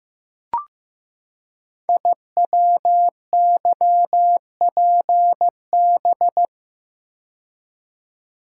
Morse code, heard as a keyed tone, about 15 wpm, 700 Hz. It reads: IWYPB
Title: Code Groups - 5 Letters - 15wpm